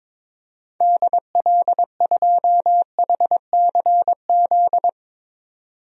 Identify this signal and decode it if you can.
Morse code, heard as a keyed tone, about 22 wpm, 700 Hz. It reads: DL2HCZ
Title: DXCC-100 Call Signs - 6 Characters - 22wpm